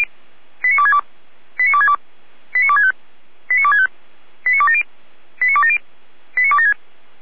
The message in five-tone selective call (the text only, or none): 89262 89262 89265 89265 89270 89270 89276